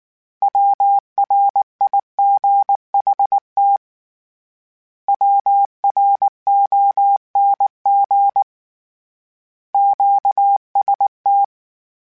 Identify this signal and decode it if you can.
Morse code, heard as a keyed tone, about 19 wpm, 800 Hz. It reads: WRIGHT WRONG QST
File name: Short Runs / Short Runs-mp3s / Words @18